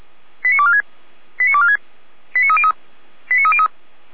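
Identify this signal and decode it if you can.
Five-tone selective call (ZVEI), heard as quick runs of five tones, tones of about 80 ms. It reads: 89246 89246 89303 89303
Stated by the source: ZVEI five-tone call received on SDR.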